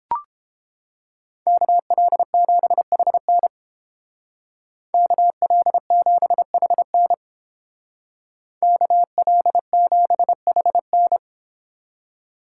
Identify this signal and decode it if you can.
Morse code, heard as a keyed tone, about 30 wpm, 700 Hz. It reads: KL7HN KL7HN KL7HN